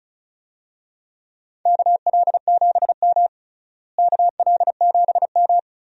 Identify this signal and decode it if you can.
Morse code, heard as a keyed tone, about 34 wpm, 700 Hz. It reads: KL7M KL7M
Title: DXCC-100 Call Signs - 4 Characters - Repeated 3x - 35wpm